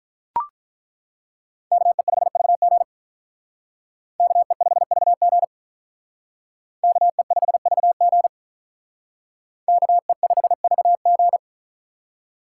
Morse code, heard as a keyed tone, about 43 wpm, 700 Hz. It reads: KE5VG KE5VG KE5VG KE5VG
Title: Speed Racing - USA 2x2 Call Signs 35wpm